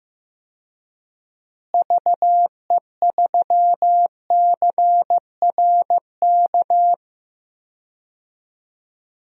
Morse code, heard as a keyed tone, about 15 wpm, 700 Hz. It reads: VE3CRK
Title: DXCC-100 Call Signs - 6 Characters - Repeated 3x - 15wpm